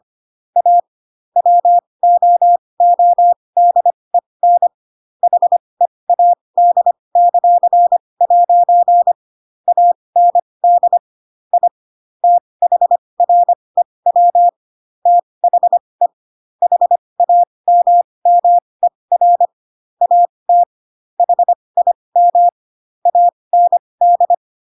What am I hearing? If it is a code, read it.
Morse code, heard as a keyed tone, about 25 wpm, spaced as for 20 wpm, 700 Hz. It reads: A WOODEN HEAD;' AND I THREW THE HAMMER AT HIM AND
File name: Pinocchio / Pinocchio ch7 15 25 ik1hge